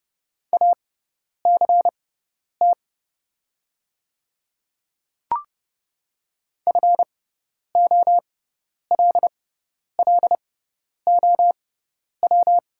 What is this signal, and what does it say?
Morse code, heard as a keyed tone, about 31 wpm, spaced as for 10 wpm, 700 Hz. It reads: ACT FOLLOW
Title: 6x Character Spacing - Top 200 Words - 30wpm